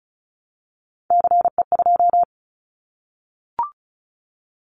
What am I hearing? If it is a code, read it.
Morse code, heard as a keyed tone, about 34 wpm, 700 Hz. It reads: CE2
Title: Callsign Prefixes for Top 100 DXCC - 35wpm